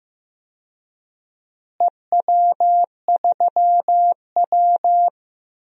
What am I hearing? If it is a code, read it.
Morse code, heard as a keyed tone, about 15 wpm, 700 Hz. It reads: EW3W
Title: DXCC-100 Call Signs - 4 Characters - Repeated 2x - 15wpm